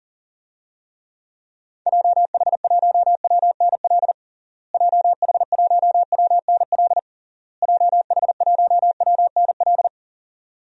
Morse code, heard as a keyed tone, about 40 wpm, 700 Hz. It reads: JH1WNL JH1WNL JH1WNL
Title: DXCC-100 Call Signs - 6 Characters - Repeated 3x - 40wpm